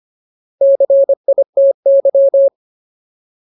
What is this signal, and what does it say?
Morse code, heard as a keyed tone, about 25 wpm, 550 Hz. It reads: CITY
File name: Ultimate Word List / 25WPM Ultimate